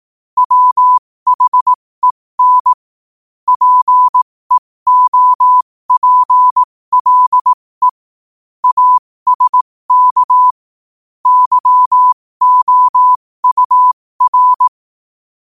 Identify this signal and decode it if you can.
Morse code, heard as a keyed tone, about 18 wpm, spaced as for 15 wpm, 1000 Hz. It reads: WHEN PEOPLE ASK YOUR
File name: qotdfiles / 15wpm